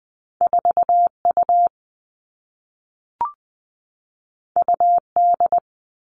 Morse code, heard as a keyed tone, about 20 wpm, 700 Hz. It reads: 4 U U D
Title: Two Challenging Letter-Number Combinations 20wpm